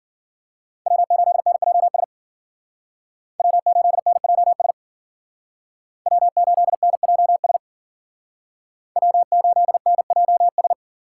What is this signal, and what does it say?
Morse code, heard as a keyed tone, about 52 wpm, 700 Hz. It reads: W8NJS W8NJS W8NJS W8NJS